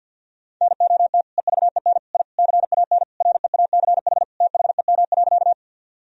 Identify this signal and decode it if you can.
Morse code, heard as a keyed tone, about 52 wpm, 700 Hz. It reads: NOT EVEN I CAN REACH THEM.
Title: Sentences from Top 1000 Words - Review Part 2 - 50wpm